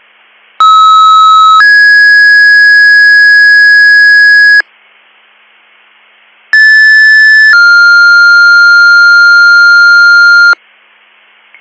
Two-tone page, heard as a one-second tone, then a three-second tone, about 1290 then 1650 Hz, about 1750 then 1400 Hz. It is heard from 0.6 to 10.5 s.